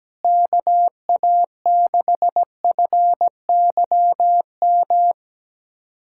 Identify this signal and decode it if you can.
Morse code, heard as a keyed tone, about 17 wpm, 700 Hz. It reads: KA6FYM
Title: USA Callsign DE Callsign - 6 Characters - 17wpm